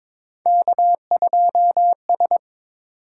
Morse code, heard as a keyed tone, about 23 wpm, 700 Hz. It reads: K2S
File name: practice / 22wpm / Three Letter-Number Combinations - 22wpm